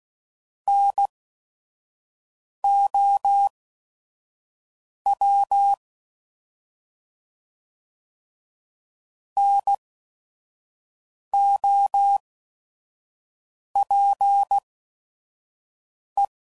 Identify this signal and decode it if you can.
Morse code, heard as a keyed tone, about 16 wpm, spaced as for 5 wpm, 800 Hz. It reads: NOW NOPE
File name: Prefixes / By First Letter / Ns